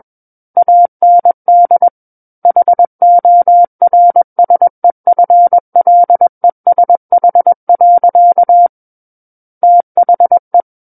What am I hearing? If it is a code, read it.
Morse code, heard as a keyed tone, about 21 wpm, 700 Hz. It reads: AND HORSEFLESH. THE
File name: Book One / 12-Chapter 12 - 21 WPM